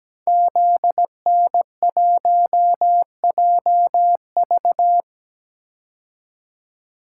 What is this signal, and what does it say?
Morse code, heard as a keyed tone, about 17 wpm, 700 Hz. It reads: ZN1JV
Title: Code Groups - 5 Letter:Numbers - 17wpm